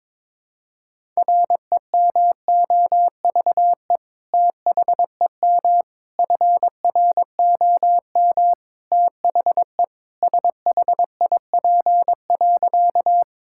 Morse code, heard as a keyed tone, about 22 wpm, 700 Hz. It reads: REMOVE THEM FROM THE SHIP.
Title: Sentences from Top 1000 Words - Review Part 5 - 22wpm